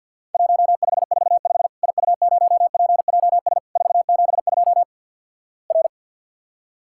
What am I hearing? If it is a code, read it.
Morse code, heard as a keyed tone, about 50 wpm, 700 Hz. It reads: 1545 IU0PJS 472 R
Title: Worked All Europe DX Contest - QTC Exchange with Repeats - 50wpm